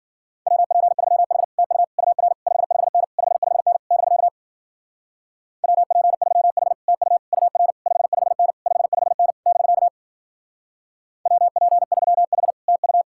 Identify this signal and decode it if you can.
Morse code, heard as a keyed tone, about 53 wpm, 700 Hz. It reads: WP3H TU UR 55N 55N <BK> WP3H TU UR 55N 55N <BK> WP3H TU